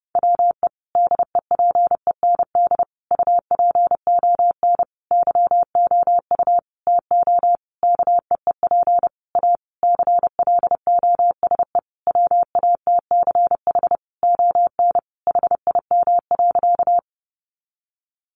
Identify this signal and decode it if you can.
Morse code, heard as a keyed tone, about 30 wpm, 700 Hz. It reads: WE DEPEND UPON YOU TO KEEP A CLOSE WATCH ON HIM.
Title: Sentences from Top 1300 Words 30wpm